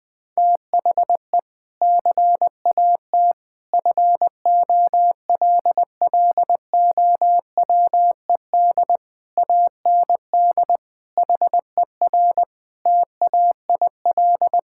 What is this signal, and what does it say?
Morse code, heard as a keyed tone, about 20 wpm, 700 Hz. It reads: THE CAT FOLLOWED AND HER TAIL
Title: Sentences with 2-Word Phrases - Spoken Beginning - 20wpm